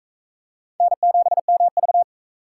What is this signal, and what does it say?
Morse code, heard as a keyed tone, about 38 wpm, 700 Hz. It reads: N7MV